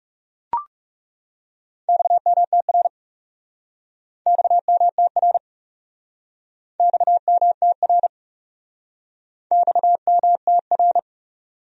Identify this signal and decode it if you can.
Morse code, heard as a keyed tone, about 37 wpm, 700 Hz. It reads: XMTR XMTR XMTR XMTR
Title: Speed-Racing Abbreviations - 30wpm